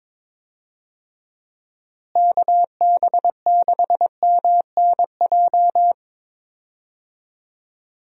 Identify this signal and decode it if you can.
Morse code, heard as a keyed tone, about 22 wpm, 700 Hz. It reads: KB6MNJ
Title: USA Call Signs - 6 Characters - Repeated 2x - 22wpm